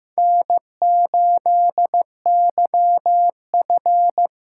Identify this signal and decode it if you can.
Morse code, heard as a keyed tone, about 15 wpm, 700 Hz. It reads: N8YF
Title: USA Call Signs - 8x Character Spacing - 15wpm